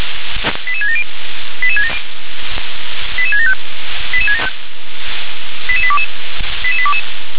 Five-tone selective call (ZVEI), heard as quick runs of five tones, tones of about 70 ms. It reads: 88670 88675 88822 88822